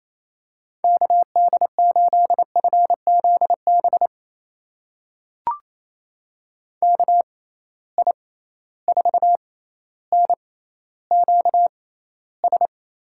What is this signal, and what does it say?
Morse code, heard as a keyed tone, about 28 wpm, spaced as for 14 wpm, 700 Hz. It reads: KD8FZB KI4NQS
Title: USA Call Signs - 6 Characters - 6x Character Spacing - 28wpm